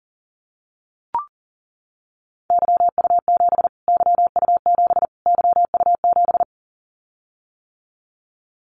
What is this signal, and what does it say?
Morse code, heard as a keyed tone, about 37 wpm, 700 Hz. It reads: YU7 YU7 YU7
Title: Callsign Prefixes for Top 25 DXCC - Repeated 3x - 40wpm